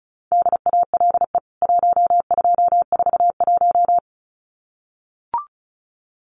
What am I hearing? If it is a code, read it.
Morse code, heard as a keyed tone, about 34 wpm, 700 Hz. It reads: DALE 1241
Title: CWT Exchange 35wpm